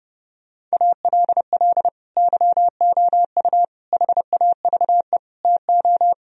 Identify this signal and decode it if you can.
Morse code, heard as a keyed tone, about 33 wpm, 700 Hz. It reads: ALL YOU HAVE TO